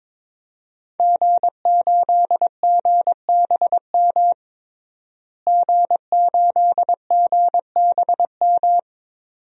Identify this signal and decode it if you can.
Morse code, heard as a keyed tone, about 22 wpm, 700 Hz. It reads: G8GBM G8GBM
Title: DXCC-100 Call Signs - 5 Characters - Repeated 2x - 22wpm